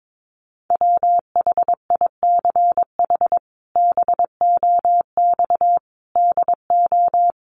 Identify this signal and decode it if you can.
Morse code, heard as a keyed tone, about 22 wpm, 700 Hz. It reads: WHICH BOX DO
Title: Sentences from Top 500 Words 22wpm